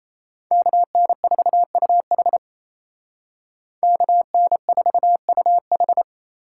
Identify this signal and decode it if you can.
Morse code, heard as a keyed tone, about 36 wpm, 700 Hz. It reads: KN4UH KN4UH